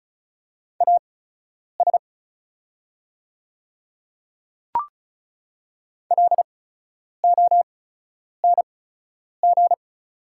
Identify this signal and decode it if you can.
Morse code, heard as a keyed tone, about 34 wpm, spaced as for 10 wpm, 700 Hz. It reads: AS LONG
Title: 8x Character Spacing - Top 100 Words - 35wpm